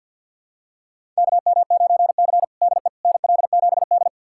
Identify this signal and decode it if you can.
Morse code, heard as a keyed tone, about 52 wpm, 700 Hz. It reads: KM9C DE NL7D
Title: USA Callsign DE Callsign - 4 Characters - 50wpm